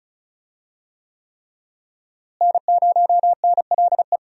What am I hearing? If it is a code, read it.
Morse code, heard as a keyed tone, about 34 wpm, 700 Hz. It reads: N0NLE
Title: USA Call Signs - 5 Characters - Repeated 3x - 35wpm